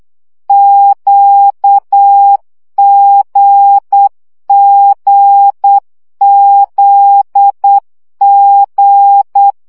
Morse code, heard as a keyed tone, about 8 wpm, 800 Hz. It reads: QGGZG
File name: CW gyakorlas g z q 7 baud